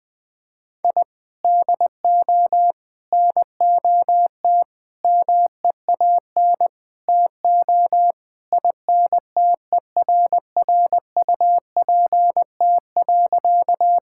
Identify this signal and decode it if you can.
Morse code, heard as a keyed tone, about 20 wpm, 700 Hz. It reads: I DO NOT MEAN TO INTERRUPT.
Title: Sentences with 5-Word Phrases - Spoken Beginning - 20wpm